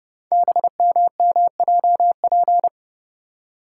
Morse code, heard as a keyed tone, about 30 wpm, 700 Hz. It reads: BMMJP
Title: Code Groups - 5 Letter:Numbers - 30wpm